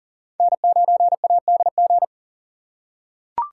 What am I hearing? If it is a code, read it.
Morse code, heard as a keyed tone, about 40 wpm, 700 Hz. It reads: N9ADG
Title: K1USN SST QSO Practice - 40wpm